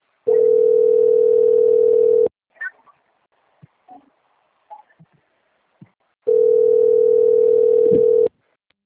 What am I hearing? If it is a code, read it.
Ringback tone (North America), 440 + 480 Hz.
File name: out- - - - - .8322